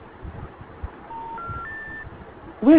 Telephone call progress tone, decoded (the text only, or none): special information tone